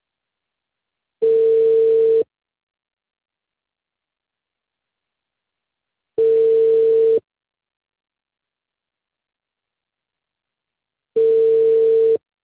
Ringback tone (China, 450 Hz).